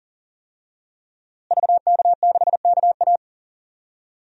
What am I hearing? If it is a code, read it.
Morse code, heard as a keyed tone, about 40 wpm, 700 Hz. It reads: VK6KA
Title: DXCC-100 Call Signs - 5 Characters - Repeated 2x - 40wpm